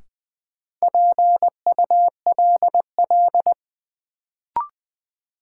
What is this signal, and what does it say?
Morse code, heard as a keyed tone, about 21 wpm, 700 Hz. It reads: PULL
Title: Sentences from Top 400 Words - New Word Drill - 20wpm